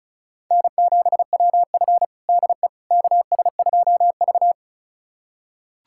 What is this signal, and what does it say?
Morse code, heard as a keyed tone, about 36 wpm, 700 Hz. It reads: N7WF DE KS2V
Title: USA Callsign DE Callsign - 4 Characters - 35wpm